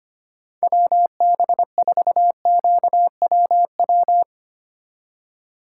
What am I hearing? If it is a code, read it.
Morse code, heard as a keyed tone, about 32 wpm, 700 Hz. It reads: WB4QWW